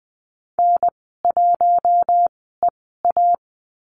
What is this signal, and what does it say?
Morse code, heard as a keyed tone, about 20 wpm, spaced as for 15 wpm, 700 Hz. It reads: N1EA
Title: 069 Lesson 12g - Letter d - Call Signs 20wpm